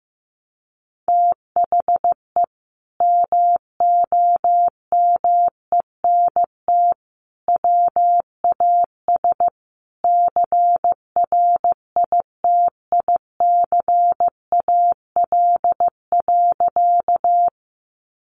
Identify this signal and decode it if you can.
Morse code, heard as a keyed tone, about 15 wpm, 700 Hz. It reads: THE MOMENT WAS CRITICAL.
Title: Sentences from Top 1100 Words 15wpm